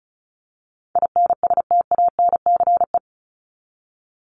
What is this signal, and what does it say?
Morse code, heard as a keyed tone, about 34 wpm, 700 Hz. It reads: INSTANCE
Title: Top 501-750 Words 35wpm